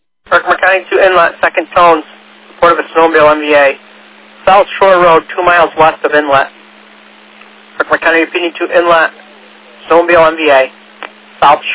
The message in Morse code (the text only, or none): none